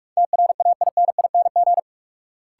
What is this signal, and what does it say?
Morse code, heard as a keyed tone, about 46 wpm, 700 Hz. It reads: TRAINING
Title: Sentences from Top 1400 Words - New Word Drill Alt - 45wpm